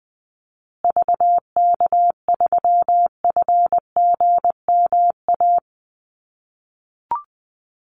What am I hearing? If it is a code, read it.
Morse code, heard as a keyed tone, about 20 wpm, 700 Hz. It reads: VK3FGMA
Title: USA Call Signs Repeated 2x 20wpm